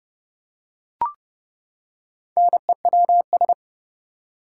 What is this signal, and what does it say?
Morse code, heard as a keyed tone, about 31 wpm, 700 Hz. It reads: NEWS
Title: Four Letter Words - 30wpm